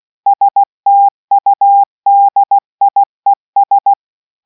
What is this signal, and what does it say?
Morse code, heard as a keyed tone, about 16 wpm, 800 Hz. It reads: STUDIES